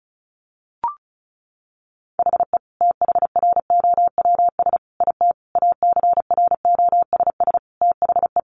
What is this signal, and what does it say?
Morse code, heard as a keyed tone, about 34 wpm, 700 Hz. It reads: HE THROWS IT ACROSS THE
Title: Additional Easy Phrases 35wpm